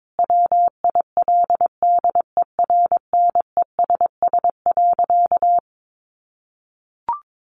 Morse code, heard as a keyed tone, about 22 wpm, 700 Hz. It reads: WILDERNESS.
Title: A Sound Of Thunder 22wpm